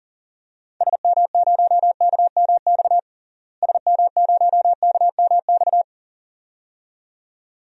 Morse code, heard as a keyed tone, about 40 wpm, 700 Hz. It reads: SM0KMX SM0KMX
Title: DXCC-100 Call Signs - 6 Characters - Repeated 2x - 40wpm